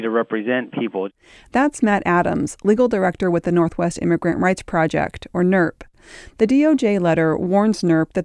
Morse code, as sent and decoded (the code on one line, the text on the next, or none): none